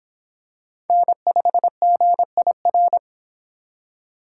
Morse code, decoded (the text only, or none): N5GIR